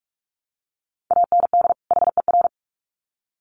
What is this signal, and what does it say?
Morse code, heard as a keyed tone, about 44 wpm, 700 Hz. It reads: AND HER
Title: Sets of 2 Words 45wpm